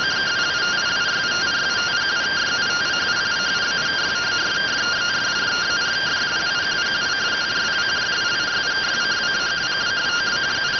Two-tone teleprinter signal, about 170 Hz shift.